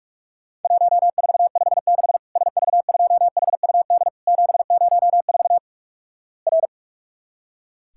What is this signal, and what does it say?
Morse code, heard as a keyed tone, about 44 wpm, 700 Hz. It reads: 1456 SV2HUD 704 R
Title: Worked All Europe DX Contest - QTC Exchange - 45wpm